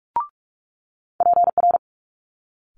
Morse code, heard as a keyed tone, about 46 wpm, 700 Hz. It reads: PR
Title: Multi-letter Phonemes Quick Fire 45wpm